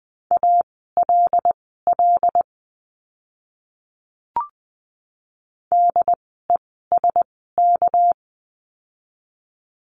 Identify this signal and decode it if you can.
Morse code, heard as a keyed tone, about 20 wpm, spaced as for 14 wpm, 700 Hz. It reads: ALL DESK
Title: 199 Lesson 32f - Letter k - Words 20wpm